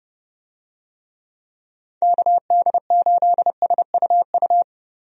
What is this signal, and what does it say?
Morse code, heard as a keyed tone, about 30 wpm, 700 Hz. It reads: KD8SUU
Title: USA Call Signs - 6 Characters - Repeated 2x - 30wpm